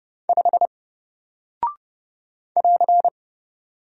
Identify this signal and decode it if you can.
Morse code, heard as a keyed tone, about 28 wpm, 700 Hz. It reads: 5 <AR>
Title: Single Character (Letter:Number:Punctuation:Prosign) Alt 30wpm